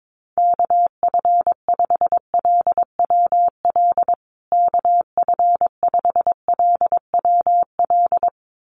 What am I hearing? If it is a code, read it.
Morse code, heard as a keyed tone, about 22 wpm, 700 Hz. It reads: KF5LWL KF5LWL
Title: USA Call Signs Repeated 2x 22wpm